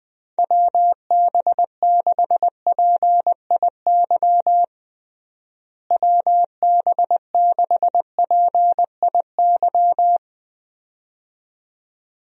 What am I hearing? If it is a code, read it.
Morse code, heard as a keyed tone, about 21 wpm, 700 Hz. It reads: WB6PIY WB6PIY
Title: USA Call Signs - 6 Characters - Repeated 2x - 20wpm